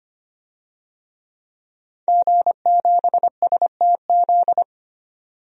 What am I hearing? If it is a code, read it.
Morse code, heard as a keyed tone, about 25 wpm, 700 Hz. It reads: G7STZ
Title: DXCC-100 Call Signs - 5 Characters - 25wpm